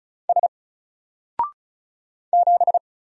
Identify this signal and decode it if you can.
Morse code, heard as a keyed tone, about 36 wpm, 700 Hz. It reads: S 7
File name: Single-Letter-Number-35wpm